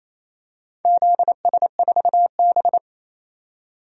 Morse code, heard as a keyed tone, about 27 wpm, 690 Hz. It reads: ZS4B
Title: DXCC-100 Call Signs - 4 Characters - 28wpm